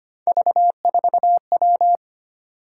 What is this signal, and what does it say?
Morse code, heard as a keyed tone, about 25 wpm, 700 Hz. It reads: V4W
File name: Three Letter-Number Combinations - 25wpm